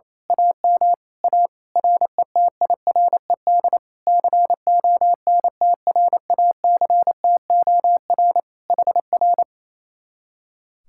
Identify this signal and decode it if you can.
Morse code, heard as a keyed tone, about 26 wpm, 700 Hz. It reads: AM A RETIRED CONTRACTOR HR
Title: Speed-Racing Common QSO Phrases Encore - 28wpm